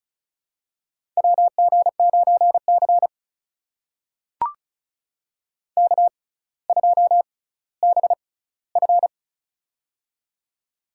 Morse code, heard as a keyed tone, about 36 wpm, spaced as for 18 wpm, 700 Hz. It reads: WG9C K2BF
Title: USA Call Signs - 4 Characters - 6x Character Spacing - 35wpm